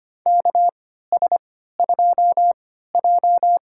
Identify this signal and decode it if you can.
Morse code, heard as a keyed tone, about 25 wpm, spaced as for 14 wpm, 700 Hz. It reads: KS2J
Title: USA Call Signs - 4 Characters - 3x Character Spacing - 25wpm